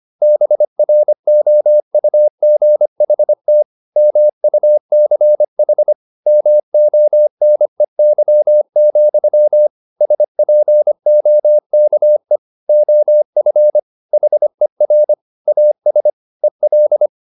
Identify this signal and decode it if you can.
Morse code, heard as a keyed tone, about 25 wpm, 600 Hz. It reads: BROUGHT MUCH MONEY, SPOKE OF HER AS EL
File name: the old man and the sea part 1 25wpm